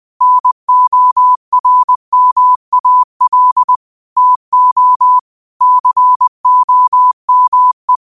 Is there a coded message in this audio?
Yes: Morse code from 0.2 to 8.0 s, heard as a keyed tone, about 20 wpm, 1000 Hz. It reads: NORMAL TO COME